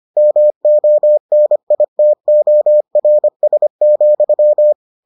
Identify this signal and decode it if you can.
Morse code, heard as a keyed tone, about 25 wpm, 600 Hz. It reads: MONITORS,